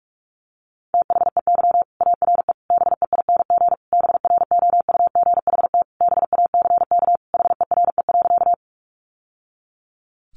Morse code, heard as a keyed tone, about 44 wpm, 700 Hz. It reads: THEY ARE BEING BROUGHT BACK HERE.
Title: Sentences from Top 600 Words 45wpm